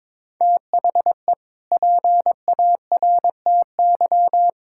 Morse code, heard as a keyed tone, about 22 wpm, 700 Hz. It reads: THE PARTY
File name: practice / 22wpm / Sentences with 2-Word Phrases - 22wpm